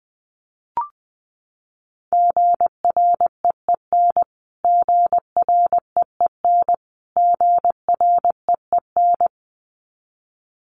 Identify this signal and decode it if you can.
Morse code, heard as a keyed tone, about 20 wpm, 700 Hz. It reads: GREEN GREEN GREEN
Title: Top 500 Words Repeated 20wpm